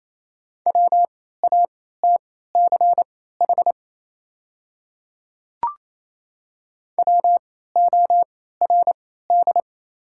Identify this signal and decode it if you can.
Morse code, heard as a keyed tone, about 28 wpm, spaced as for 16 wpm, 700 Hz. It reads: WATCH WORD
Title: 3x Character Spacing - Top 500 Words - 28wpm